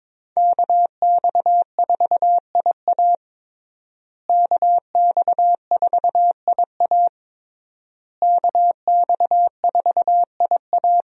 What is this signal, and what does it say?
Morse code, heard as a keyed tone, about 22 wpm, 700 Hz. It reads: KX4IA KX4IA KX4IA
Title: USA Call Signs - 5 Characters - Repeated 3x - 22wpm